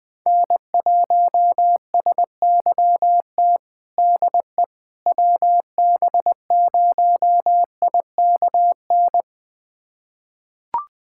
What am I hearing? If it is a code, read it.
Morse code, heard as a keyed tone, about 20 wpm, 700 Hz. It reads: N1SYT DE WB0IKN
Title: USA Callsign DE Callsign 20wpm